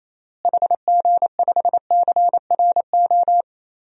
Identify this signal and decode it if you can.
Morse code, heard as a keyed tone, about 28 wpm, 700 Hz. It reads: HG5CRO
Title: DXCC-100 Call Signs - 6 Characters - Repeated 3x - 28wpm